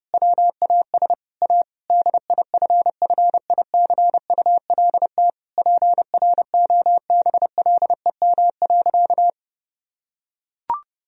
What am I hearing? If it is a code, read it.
Morse code, heard as a keyed tone, about 32 wpm, 700 Hz. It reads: WAS A DIFFICULT PROBLEM.
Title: Sentences from Top 1000 Words - Review Part 2 - 30wpm